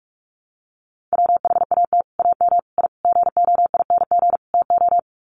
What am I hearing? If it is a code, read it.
Morse code, heard as a keyed tone, about 44 wpm, 700 Hz. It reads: WHAT AM I GOING TO
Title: Sets of 5 Words Encore 45wpm